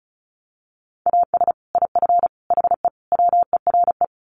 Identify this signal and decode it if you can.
Morse code, heard as a keyed tone, about 34 wpm, 700 Hz. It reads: AS IF HE WERE
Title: Sets of 4 Words 35wpm